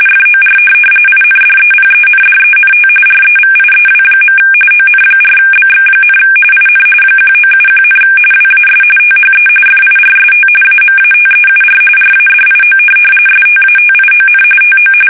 RTTY, about 45.45 baud. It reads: FOR INDEPENDENT REVIEW. ON4BX, IN 1971, WAS THE FIRST AMATEUR RADIO STATION TO
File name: fsk850 / teletype-RTTY